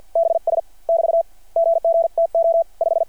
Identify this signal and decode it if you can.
Morse code, heard as a keyed tone, about 53 wpm, 670 Hz. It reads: BS <BT> GG TO 5
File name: archived / CRSnet